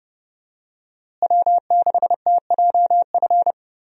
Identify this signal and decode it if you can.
Morse code, heard as a keyed tone, about 30 wpm, 700 Hz. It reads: W6TJF
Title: USA Call Signs - 5 Characters - Repeated 3x - 30wpm